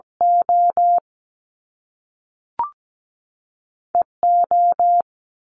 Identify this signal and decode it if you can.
Morse code, heard as a keyed tone, about 17 wpm, 700 Hz. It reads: O E O
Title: Top Two Letter Combinations in Words 17wpm